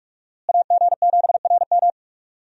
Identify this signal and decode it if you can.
Morse code, heard as a keyed tone, about 46 wpm, 700 Hz. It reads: AG7RM
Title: USA Call Signs - 5 Characters - Repeated 3x - 45wpm